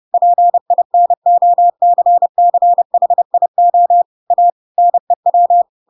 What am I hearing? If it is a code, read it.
Morse code, heard as a keyed tone, about 30 wpm, 700 Hz. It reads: PINOCCHIO A NEW
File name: Pinocchio / Pinocchio ch8 30 30 ik1hge